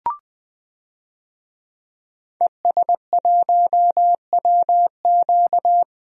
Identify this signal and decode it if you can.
Morse code, heard as a keyed tone, about 20 wpm, 700 Hz. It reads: ES1WQ
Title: DXCC-100 Call Signs - 5 Characters - 20wpm